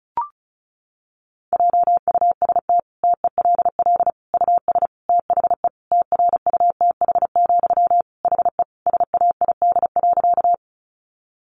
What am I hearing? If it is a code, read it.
Morse code, heard as a keyed tone, about 34 wpm, 700 Hz. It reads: JUST TELL US THE TRUTH, HE SAID.
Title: Sentences from Top 800 Words 35wpm